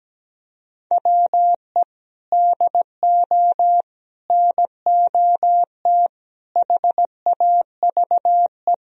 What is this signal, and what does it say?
Morse code, heard as a keyed tone, about 17 wpm, 700 Hz. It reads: WE DO NOT HAVE
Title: Sentences from Top 1000 Words - Review Part 4 - 17wpm